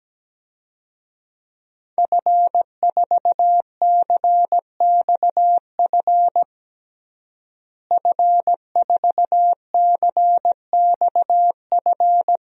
Morse code, heard as a keyed tone, about 17 wpm, 700 Hz. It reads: F4CXF F4CXF
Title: DXCC-100 Call Signs - 5 Characters - Repeated 3x - 17wpm